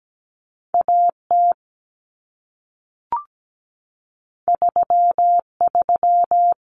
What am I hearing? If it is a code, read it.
Morse code, heard as a keyed tone, about 17 wpm, 700 Hz. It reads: AT 33
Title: Common Cut Numbers - 2-Digit 17wpm